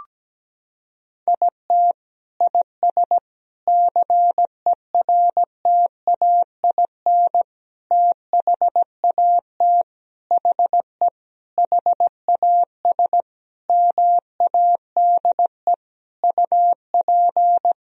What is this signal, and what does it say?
Morse code, heard as a keyed tone, about 17 wpm, 700 Hz. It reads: IT IS CERTAIN THAT HE HAS MADE UP
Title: Sentences from Top 1000 Words - Review Part 2 - 17wpm